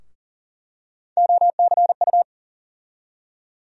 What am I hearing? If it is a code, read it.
Morse code, heard as a keyed tone, about 40 wpm, 700 Hz. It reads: OCU
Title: USA Call Sign Postfixes - 3 Letters 40wpm